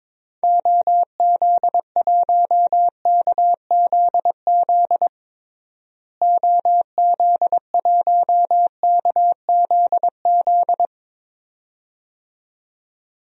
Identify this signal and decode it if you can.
Morse code, heard as a keyed tone, about 22 wpm, 700 Hz. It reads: OZ1KZZ OZ1KZZ
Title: DXCC-100 Call Signs - 6 Characters - Repeated 2x - 22wpm